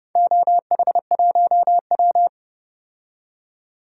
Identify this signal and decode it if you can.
Morse code, heard as a keyed tone, about 30 wpm, 700 Hz. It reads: OH1W
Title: DXCC-100 Call Signs - 4 Characters - 30wpm